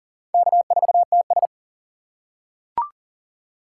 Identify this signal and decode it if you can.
Morse code, heard as a keyed tone, about 40 wpm, 700 Hz. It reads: K4TS
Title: USA Call Signs - 4 Characters - 6x Character Spacing - 40wpm